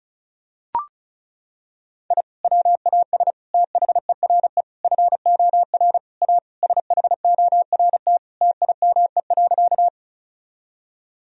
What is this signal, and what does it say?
Morse code, heard as a keyed tone, about 34 wpm, 700 Hz. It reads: I WAS THERE FOR A SHORT TIME.
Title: Sentences from Top 1000 Words - Review Part 2 - 35wpm